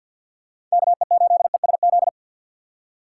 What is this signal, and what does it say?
Morse code, heard as a keyed tone, about 51 wpm, 700 Hz. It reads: KE8ESZ
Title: USA Call Signs - 4x Character Spacing - 50wpm